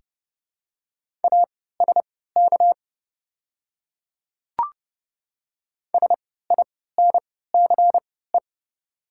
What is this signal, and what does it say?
Morse code, heard as a keyed tone, about 30 wpm, spaced as for 17 wpm, 700 Hz. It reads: ASK SINCE
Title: 3x Character Spacing - Top 500 Words - 30wpm